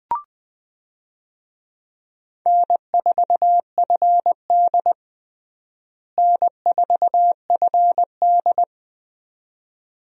Morse code, heard as a keyed tone, about 20 wpm, 700 Hz. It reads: N4FD N4FD
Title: USA Call Signs - 4 Characters - Repeated 2x - 20wpm